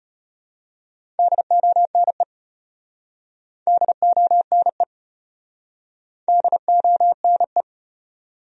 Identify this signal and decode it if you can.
Morse code, heard as a keyed tone, about 33 wpm, 700 Hz. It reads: DONE DONE DONE